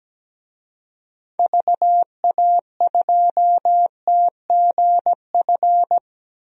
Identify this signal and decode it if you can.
Morse code, heard as a keyed tone, about 17 wpm, 700 Hz. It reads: VA2TGF
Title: DXCC-100 Call Signs - 6 Characters - Repeated 3x - 17wpm